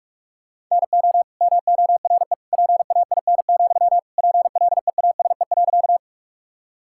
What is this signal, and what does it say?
Morse code, heard as a keyed tone, about 43 wpm, 700 Hz. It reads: NO MORE PAIN, PLEASE.
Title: Sentences from Top 1600 Words 45wpm